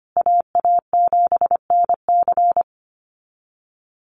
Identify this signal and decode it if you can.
Morse code, heard as a keyed tone, about 25 wpm, 700 Hz. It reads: AA7NC
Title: USA Call Signs Repeated 3x 25wpm